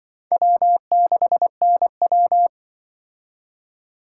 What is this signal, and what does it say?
Morse code, heard as a keyed tone, about 24 wpm, 700 Hz. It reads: W6NW
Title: Speed Racing - USA 1x2 Call Signs 20wpm